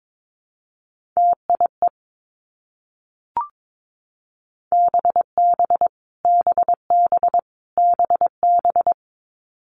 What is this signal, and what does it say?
Morse code, heard as a keyed tone, about 22 wpm, 700 Hz. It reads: TIE BB BB BB
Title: Multi-letter Phonemes Repeated 3x 22wpm